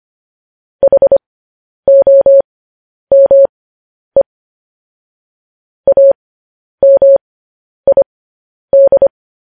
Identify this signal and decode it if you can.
Morse code, heard as a keyed tone, about 25 wpm, spaced as for 10 wpm, 550 Hz. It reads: HOME AMID